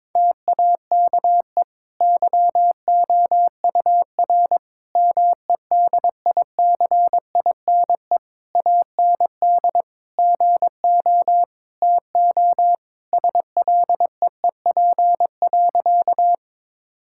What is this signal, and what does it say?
Morse code, heard as a keyed tone, about 22 wpm, 700 Hz. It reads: TAKE YOUR MEDICINE AND GO TO SLEEP.
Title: Sentences from Top 1000 Words - Review Part 6 - 22wpm